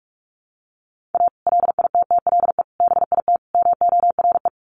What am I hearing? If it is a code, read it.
Morse code, heard as a keyed tone, about 44 wpm, 700 Hz. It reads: A LITTLE BIT MORE
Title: Sets of 4 Words 45wpm